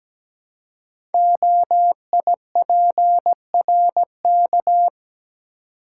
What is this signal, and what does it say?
Morse code, heard as a keyed tone, about 17 wpm, 700 Hz. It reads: OIPRK
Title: Code Groups - 5 Letters - 17wpm